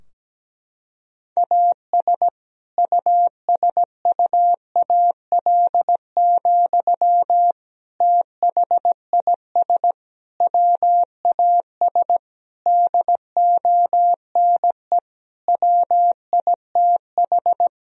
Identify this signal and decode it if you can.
Morse code, heard as a keyed tone, about 17 wpm, 700 Hz. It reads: AS USUAL, THIS WAS DONE WITH
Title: Sentences from Top 1000 Words - Review Part 2 - 17wpm